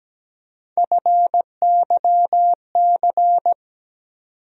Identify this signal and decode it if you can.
Morse code, heard as a keyed tone, about 17 wpm, 700 Hz. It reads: FYC
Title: USA Call Sign Postfixes - 3 Letters 17wpm